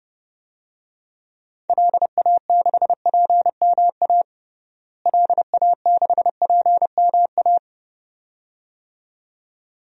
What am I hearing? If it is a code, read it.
Morse code, heard as a keyed tone, about 30 wpm, 700 Hz. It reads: LA6PMA LA6PMA
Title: DXCC-100 Call Signs - 6 Characters - Repeated 2x - 30wpm